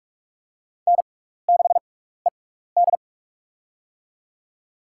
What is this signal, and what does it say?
Morse code, heard as a keyed tone, about 44 wpm, spaced as for 15 wpm, 700 Hz. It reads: N6ED
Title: USA Call Signs - 4 Characters - 6x Character Spacing - 45wpm